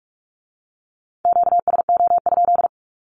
Morse code, heard as a keyed tone, about 46 wpm, 700 Hz. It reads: QSO?
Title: Q-Codes with Spoken Meaning - 45wpm